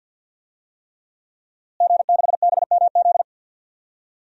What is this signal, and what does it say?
Morse code, heard as a keyed tone, about 53 wpm, 700 Hz. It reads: G6BMZ
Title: DXCC-100 Call Signs - 5 Characters - Repeated 3x - 50wpm